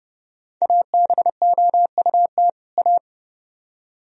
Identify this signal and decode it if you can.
Morse code, heard as a keyed tone, about 30 wpm, 700 Hz. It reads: ABOUT A